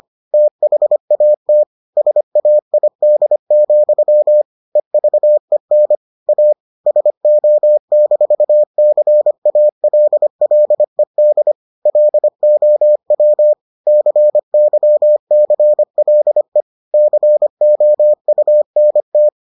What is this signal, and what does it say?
Morse code, heard as a keyed tone, about 25 wpm, 600 Hz. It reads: THAT SAID, EVEN A SO-CALLED LOW CYCLE COUNT